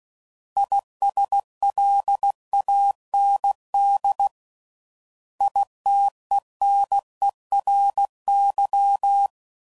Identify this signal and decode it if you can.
Morse code, heard as a keyed tone, about 16 wpm, 800 Hz. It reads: ISLAND ITENERY
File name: Prefixes / By First Letter / Is